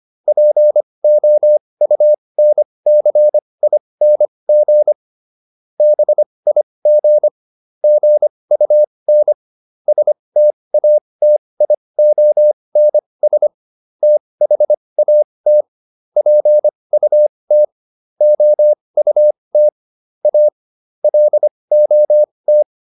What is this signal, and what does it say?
Morse code, heard as a keyed tone, about 25 wpm, spaced as for 20 wpm, 600 Hz. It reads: POUNCING BIG GUN STATIONS THAT PUT OUT A LOT